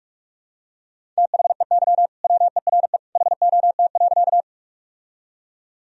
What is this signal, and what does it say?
Morse code, heard as a keyed tone, about 45 wpm, 700 Hz. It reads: THEY WERE HOT.
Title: Sentences from Top 1000 Words - Review Part 3 - 45wpm